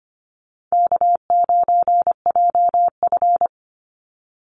Morse code, heard as a keyed tone, about 25 wpm, 700 Hz. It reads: K9JF
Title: Easy Call Signs - 25wpm